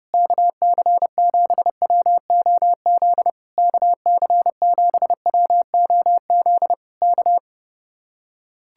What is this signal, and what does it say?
Morse code, heard as a keyed tone, about 30 wpm, 700 Hz. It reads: KC7WOZ KC7WOZ K